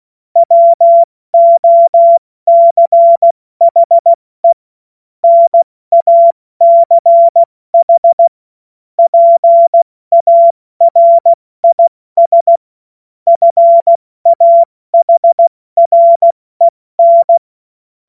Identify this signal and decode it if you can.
Morse code, heard as a keyed tone, about 16 wpm, 680 Hz. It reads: WOCHE NACH PARIS FAHREN
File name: Morsekurs / Witze / Witz 19. Oktober 2021, 70 F80